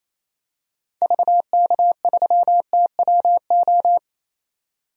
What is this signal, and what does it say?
Morse code, heard as a keyed tone, about 28 wpm, 700 Hz. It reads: VK3TWO
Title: DXCC-100 Call Signs - 6 Characters - 28wpm